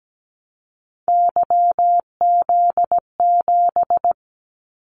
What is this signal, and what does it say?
Morse code, heard as a keyed tone, about 17 wpm, 700 Hz. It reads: YZ7
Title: Callsign Prefixes for Top 25 DXCC - 17wpm